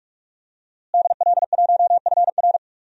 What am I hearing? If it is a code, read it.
Morse code, heard as a keyed tone, about 42 wpm, 700 Hz. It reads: DL1FR
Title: DXCC-100 Call Signs - 5 Characters - Repeated 2x - 45wpm